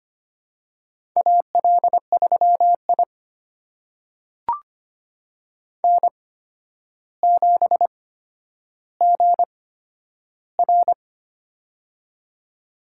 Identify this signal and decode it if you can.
Morse code, heard as a keyed tone, about 25 wpm, spaced as for 11 wpm, 700 Hz. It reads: AL3I N7GR
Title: USA Call Signs - 4 Characters - 8x Character Spacing - 25wpm